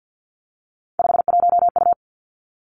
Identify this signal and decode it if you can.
Morse code, heard as a keyed tone, about 50 wpm, 700 Hz. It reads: 51U
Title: Three Challenging Letter-Number Combinations 50wpm